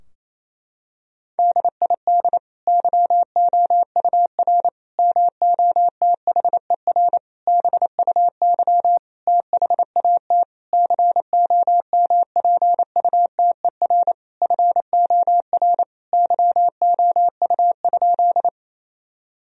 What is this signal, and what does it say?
Morse code, heard as a keyed tone, about 28 wpm, 700 Hz. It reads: DID YOUR MOTHER BUY THAT COMPUTER FOR YOU?
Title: Sentences from Top 1000 Words - Review Part 3 - 28wpm